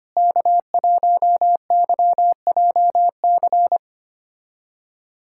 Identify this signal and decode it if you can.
Morse code, heard as a keyed tone, about 25 wpm, 700 Hz. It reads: K1YJC
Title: USA Callsign DE Callsign - 5 Characters - 25wpm